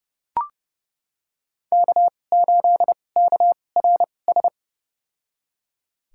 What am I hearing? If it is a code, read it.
Morse code, heard as a keyed tone, about 29 wpm, spaced as for 21 wpm, 690 Hz. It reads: K8KRS
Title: USA Call Signs - 5 Characters - 2x Character Spacing - 30wpm